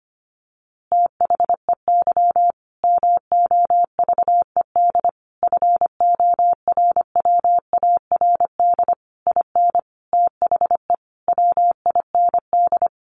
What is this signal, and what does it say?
Morse code, heard as a keyed tone, about 25 wpm, 700 Hz. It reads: THEY MOVED FORWARD IN THE WIND
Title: A Sound Of Thunder 25wpm